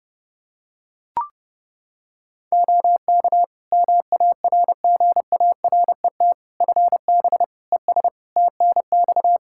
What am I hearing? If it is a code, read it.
Morse code, heard as a keyed tone, about 30 wpm, 700 Hz. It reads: OK MARGARET FB ES TNX
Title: Practice QSO - Exchange 2 - Info - 30wpm